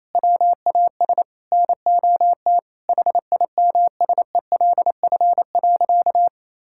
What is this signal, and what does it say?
Morse code, heard as a keyed tone, about 28 wpm, 700 Hz. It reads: WAS NOT HIMSELF.
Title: Sentences from Top 1600 Words 28wpm